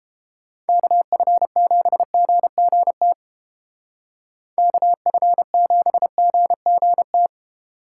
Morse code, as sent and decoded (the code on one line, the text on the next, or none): -.- ..-. --... --. --. - / -.- ..-. --... --. --. -
KF7GGT KF7GGT